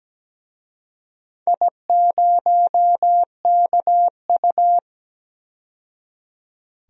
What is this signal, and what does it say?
Morse code, heard as a keyed tone, about 17 wpm, 700 Hz. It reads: I0KU
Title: DXCC-100 Call Signs - 4 Characters - 17wpm